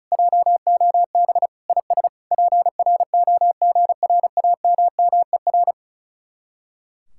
Morse code, heard as a keyed tone, about 46 wpm, 700 Hz. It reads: JOB IS PROGRAMMER